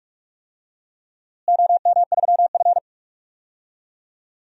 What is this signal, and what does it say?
Morse code, heard as a keyed tone, about 47 wpm, 700 Hz. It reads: OM3F
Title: DXCC-100 Call Signs - 4 Characters - Repeated 2x - 45wpm